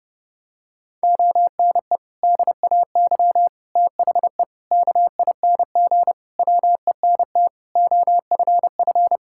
Morse code, heard as a keyed tone, about 30 wpm, 700 Hz. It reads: ONE DAY THE KING WENT OFF
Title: Sentences from Top 1600 Words 30wpm